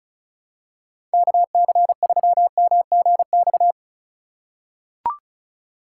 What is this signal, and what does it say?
Morse code, heard as a keyed tone, about 36 wpm, 690 Hz. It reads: KC3MGX
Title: USA Call Signs - 6 Characters - 2x Character Spacing - 35wpm